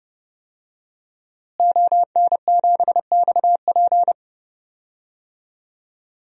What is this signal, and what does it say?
Morse code, heard as a keyed tone, about 32 wpm, 700 Hz. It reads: ON7XP
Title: DXCC-100 Call Signs - 5 Characters - Repeated 2x - 30wpm